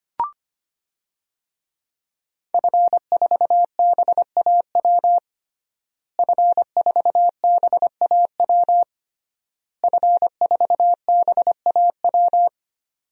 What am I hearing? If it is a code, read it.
Morse code, heard as a keyed tone, about 25 wpm, 700 Hz. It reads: F4BAW F4BAW F4BAW
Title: DXCC-100 Call Signs - 5 Characters - Repeated 3x - 25wpm